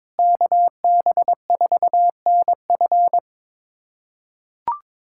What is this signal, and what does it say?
Morse code, heard as a keyed tone, about 22 wpm, 700 Hz. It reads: KB4NF
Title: USA Callsign DE Callsign - 5 Characters - 22wpm